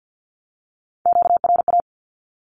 Morse code, heard as a keyed tone, about 49 wpm, 700 Hz. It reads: QRA
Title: Q-Codes 50wpm